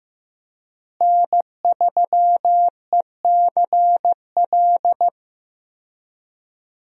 Morse code, heard as a keyed tone, about 15 wpm, 700 Hz. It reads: N3ECL
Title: USA Call Signs - 5 Characters - Repeated 3x - 15wpm